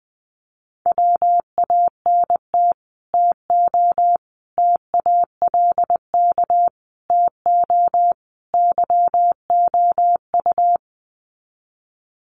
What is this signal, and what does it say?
Morse code, heard as a keyed tone, about 20 wpm, 700 Hz. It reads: WANT TO TALK TO YOU
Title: Sets of 5 Words Encore 20wpm